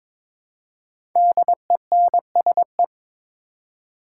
Morse code, heard as a keyed tone, about 22 wpm, 690 Hz. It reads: DENSE
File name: 5 Letter Words - First 3 Letters Given - 22wpm